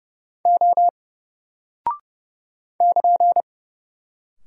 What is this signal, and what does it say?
Morse code, heard as a keyed tone, about 30 wpm, 700 Hz. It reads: O <KN>
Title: Single Character (Letter:Number:Punctuation:Prosign) Alt 30wpm